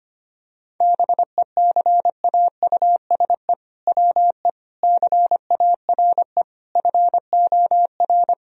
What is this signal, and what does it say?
Morse code, heard as a keyed tone, about 25 wpm, 700 Hz. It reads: BECAUSE WE CARE FOR
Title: Sentences from Top 1000 Words - Review Part 3 - 25wpm